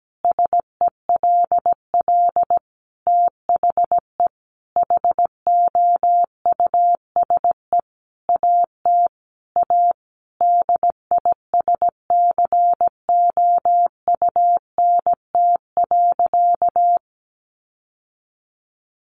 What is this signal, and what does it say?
Morse code, heard as a keyed tone, about 17 wpm, 700 Hz. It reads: SELL THE HOUSE AT A DISCOUNT.
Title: Sentences from Top 1200 Words 17wpm